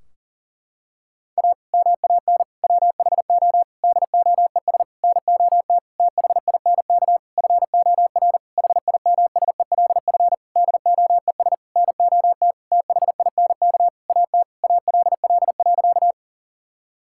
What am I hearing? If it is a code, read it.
Morse code, heard as a keyed tone, about 38 wpm, 700 Hz. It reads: A MAN WHO DOES NOT THINK FOR HIMSELF DOES NOT THINK AT ALL.
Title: Sentences from Top 1600 Words 40wpm